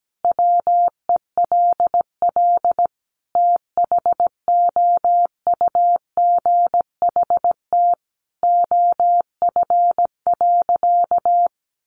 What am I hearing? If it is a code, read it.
Morse code, heard as a keyed tone, about 17 wpm, 700 Hz. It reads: WELL THOUGHT OF.
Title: Sentences from Top 300 Words 17wpm